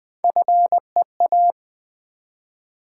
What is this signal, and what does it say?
Morse code, heard as a keyed tone, about 20 wpm, 700 Hz. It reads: FEA
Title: Build A Word - 1 Letter At A Time - 4-Letter Words - 20wpm